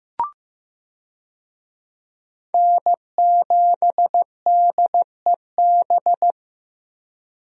Morse code, heard as a keyed tone, about 15 wpm, 700 Hz. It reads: N7DEB
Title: USA Call Signs - 5 Characters - 15wpm